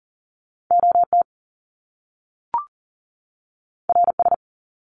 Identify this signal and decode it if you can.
Morse code, heard as a keyed tone, about 38 wpm, 700 Hz. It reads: OT RS
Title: Top Two Letter Combinations in Words 40wpm